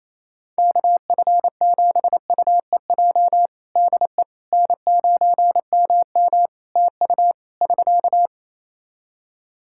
Morse code, heard as a keyed tone, about 28 wpm, 700 Hz. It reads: KF7UEJ DE N9MM TU <SK>
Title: Practice QSO - Exchange 4 - Ending - 28wpm